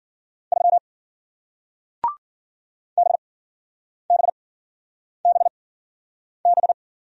Morse code, heard as a keyed tone, about 53 wpm, spaced as for 9 wpm, 700 Hz. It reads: 3 BBBB